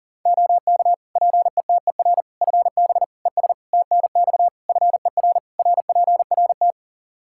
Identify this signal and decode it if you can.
Morse code, heard as a keyed tone, about 40 wpm, 700 Hz. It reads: OK PETER FB ES TNX FER RPRT
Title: K1USN SST QSO Practice 40wpm